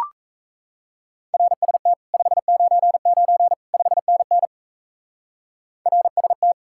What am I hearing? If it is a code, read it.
Morse code, heard as a keyed tone, about 35 wpm, 700 Hz. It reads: RST 599 5NN RST
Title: Speed-Racing Common QSO Phrases - 28wpm